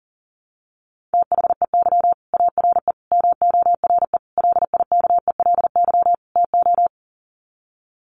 Morse code, heard as a keyed tone, about 40 wpm, 700 Hz. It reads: THEY ARE MORE LIKELY TO
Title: Sets of 5 Words Encore2 40wpm